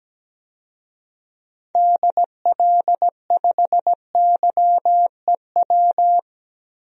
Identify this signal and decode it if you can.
Morse code, heard as a keyed tone, about 17 wpm, 700 Hz. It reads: DL5YEW
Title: DXCC-100 Call Signs - 6 Characters - 17wpm